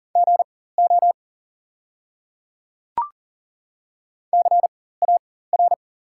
Morse code, heard as a keyed tone, about 42 wpm, spaced as for 19 wpm, 690 Hz. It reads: GO CAR
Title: 4x Character Spacing - Top 300 Words - 40wpm